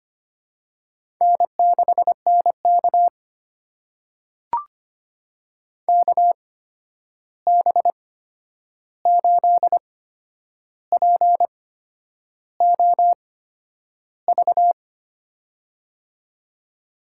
Morse code, heard as a keyed tone, about 25 wpm, spaced as for 9 wpm, 700 Hz. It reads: N6NK KB8POV
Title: USA Call Signs - 8x Character Spacing - 25wpm